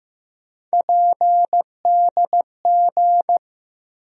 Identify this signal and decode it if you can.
Morse code, heard as a keyed tone, about 15 wpm, 700 Hz. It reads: PDG
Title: USA Call Sign Postfixes - 3 Letters 15wpm